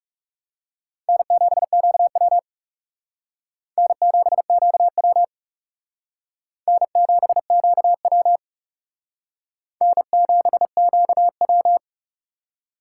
Morse code, heard as a keyed tone, about 36 wpm, 700 Hz. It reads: N7QW N7QW N7QW N7QW
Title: Speed Racing - USA 1x2 Call Signs 30wpm